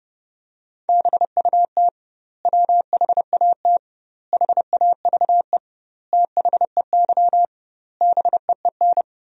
Morse code, heard as a keyed tone, about 30 wpm, 700 Hz. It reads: BUT WHAT HAVE THEY BEEN
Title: 2x Word-Spacing - Sentences from Top 100 Words - 30wpm